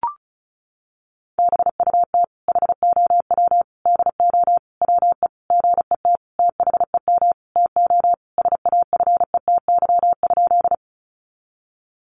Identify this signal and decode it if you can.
Morse code, heard as a keyed tone, about 34 wpm, 700 Hz. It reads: BUT HOW DO WE GET THEM TO SAFETY?
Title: Sentences from Top 700 Words 35wpm